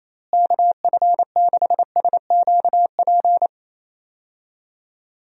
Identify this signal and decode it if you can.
Morse code, heard as a keyed tone, about 28 wpm, 700 Hz. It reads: KF6SQP
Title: USA Call Signs - 6 Characters - Repeated 2x - 28wpm